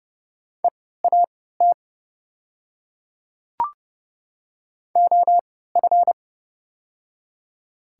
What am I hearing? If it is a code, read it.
Morse code, heard as a keyed tone, about 29 wpm, spaced as for 17 wpm, 690 Hz. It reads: EAT OF
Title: 3x Character Spacing - Top 500 Words - 30wpm